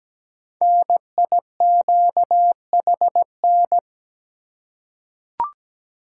Morse code, heard as a keyed tone, about 17 wpm, 700 Hz. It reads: NIQHN
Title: Code Groups - 5 Letter:Numbers - 17wpm